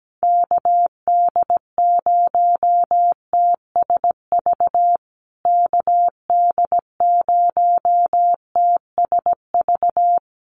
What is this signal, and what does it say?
Morse code, heard as a keyed tone, about 17 wpm, 700 Hz. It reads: KD0TSV KD0TSV
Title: USA Call Signs Repeated 2x 17wpm